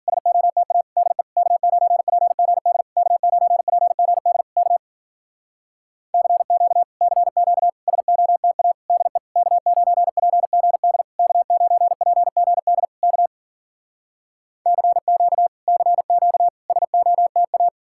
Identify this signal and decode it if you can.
Morse code, heard as a keyed tone, about 52 wpm, 700 Hz. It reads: SOTA DE K9PGD K9PGD K CQ CQ SOTA DE K9PGD K9PGD K CQ CQ SOTA